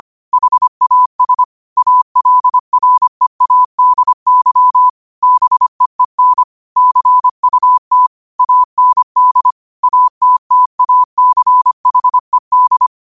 Morse code, heard as a keyed tone, about 25 wpm, 1000 Hz. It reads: HAS ALREADY BEEN CUT AND ATTACHED